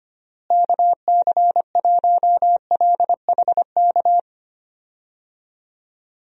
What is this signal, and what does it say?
Morse code, heard as a keyed tone, about 25 wpm, 700 Hz. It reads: KC1LHK
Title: USA Call Signs - 6 Characters - Repeated 3x - 25wpm